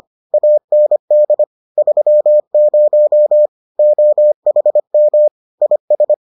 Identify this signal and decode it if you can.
Morse code, heard as a keyed tone, about 25 wpm, 600 Hz. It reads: AND 30 OHM IS